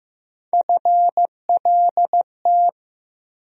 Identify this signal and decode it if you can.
Morse code, heard as a keyed tone, about 15 wpm, 700 Hz. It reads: FLT
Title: USA Call Sign Postfixes - 3 Letters 15wpm